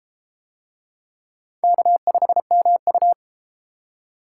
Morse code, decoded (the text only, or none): K5MU